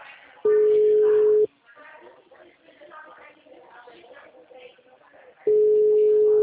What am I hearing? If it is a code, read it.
Ringback tone (Europe, 425 Hz), with speech.
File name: out- -2005- - - .9625